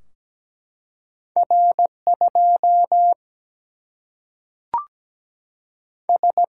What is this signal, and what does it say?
Morse code, heard as a keyed tone, about 17 wpm, 700 Hz. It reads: R 2 S
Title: Single-Letter Single-Number 17wpm